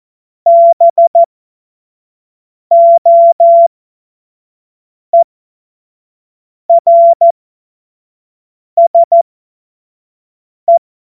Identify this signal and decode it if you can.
Morse code, heard as a keyed tone, about 14 wpm, spaced as for 5 wpm, 680 Hz. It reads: BOERSE